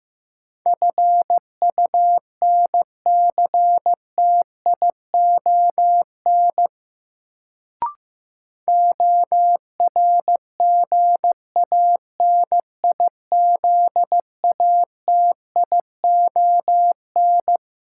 Morse code, heard as a keyed tone, about 15 wpm, 700 Hz. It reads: FUNCTION ORGANIZATION
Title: Sentences from Top 1100 Words - New Word Drill Alt - 15wpm